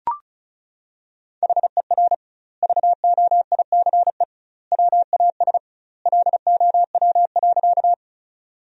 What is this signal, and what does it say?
Morse code, heard as a keyed tone, about 36 wpm, 700 Hz. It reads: HER VOICE WAS LOW.
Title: 2x Word-Spacing - Sentences from Top 500 Words - 35wpm